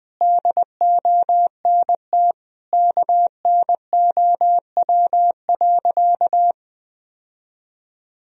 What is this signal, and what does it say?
Morse code, heard as a keyed tone, about 20 wpm, 700 Hz. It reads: DONT KNOW.
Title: Sentences from Top 1000 Words - Review Part 3 - 20wpm